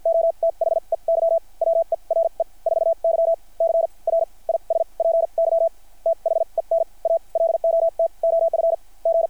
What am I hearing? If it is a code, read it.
Morse code, heard as a keyed tone, about 53 wpm, spaced as for 39 wpm, 660 Hz. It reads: O THEY WERE 4Y CRISPY THEN A LOT OF G
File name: archived / CRSnet